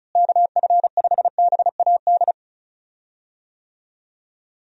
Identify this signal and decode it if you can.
Morse code, heard as a keyed tone, about 36 wpm, 700 Hz. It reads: KF5BAD
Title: USA Call Signs - 6 Characters - Repeated 2x - 35wpm